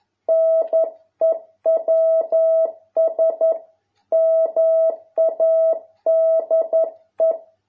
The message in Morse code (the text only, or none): NEWS MADE